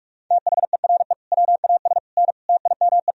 Morse code, heard as a keyed tone, about 46 wpm, 700 Hz. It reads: THERE WAS N TIME
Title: Sentences from Top 400 Words and Abbreviations 45wpm